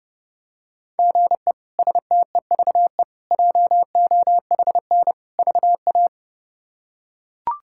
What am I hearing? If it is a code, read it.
Morse code, heard as a keyed tone, about 31 wpm, 700 Hz. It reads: GE STEVE JOHN VA
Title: K1USN SST QSO Practice - 30wpm